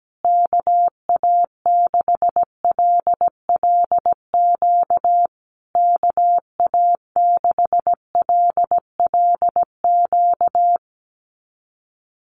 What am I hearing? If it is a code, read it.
Morse code, heard as a keyed tone, about 17 wpm, 700 Hz. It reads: KA6LLQ KA6LLQ
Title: USA Call Signs Repeated 3x 17wpm